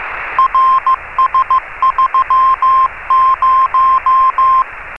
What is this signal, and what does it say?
Morse code, heard as a keyed tone, about 15 wpm, 1050 Hz, with radio noise. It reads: RS30
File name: RS30-CW-Beacon